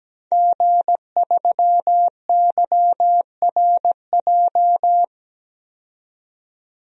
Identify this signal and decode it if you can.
Morse code, heard as a keyed tone, about 17 wpm, 700 Hz. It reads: G3YRJ
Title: DXCC-100 Call Signs - 5 Characters - 17wpm